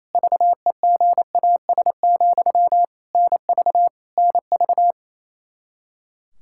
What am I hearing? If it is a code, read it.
Morse code, heard as a keyed tone, about 27 wpm, 700 Hz. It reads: VEGAS, NV NV
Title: K1USN SST QSO Practice 28wpm